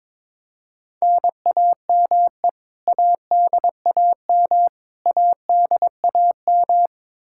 Morse code, heard as a keyed tone, about 22 wpm, 700 Hz. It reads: NAME ADAM ADAM
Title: K1USN SST QSO Practice 22wpm